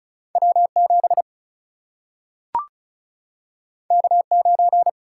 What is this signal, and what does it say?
Morse code, heard as a keyed tone, about 36 wpm, 700 Hz. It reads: W7 K9
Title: Single-Letter Single-Number 35wpm